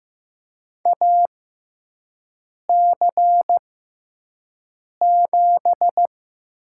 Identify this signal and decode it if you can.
Morse code, heard as a keyed tone, about 15 wpm, spaced as for 5 wpm, 700 Hz. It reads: AC7
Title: USA Call Signs - 5 Characters - 6x Character Spacing - 15wpm